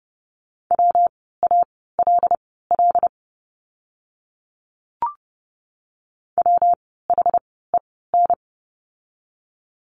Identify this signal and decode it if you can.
Morse code, heard as a keyed tone, about 30 wpm, spaced as for 17 wpm, 700 Hz. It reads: WALL WHEN
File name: course / 30:10wpm / 115 Lesson 19f - Letter w - Words 30wpm